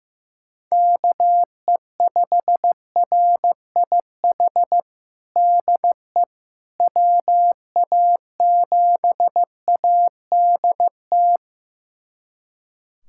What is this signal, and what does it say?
Morse code, heard as a keyed tone, about 15 wpm, 700 Hz. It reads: KE5RIH DE WA7ADT
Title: USA Callsign DE Callsign - 6 Characters - 15wpm